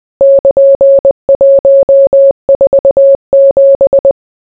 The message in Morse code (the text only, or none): <KN>147